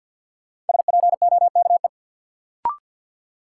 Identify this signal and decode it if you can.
Morse code, heard as a keyed tone, about 53 wpm, 700 Hz. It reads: SPOKE
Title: Sentences from Top 1200 Words - New Word Drill - 50wpm